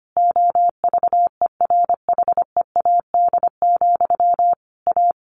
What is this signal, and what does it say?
Morse code, heard as a keyed tone, about 25 wpm, 700 Hz. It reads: OVERHEAD, A
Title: A Sound Of Thunder 25wpm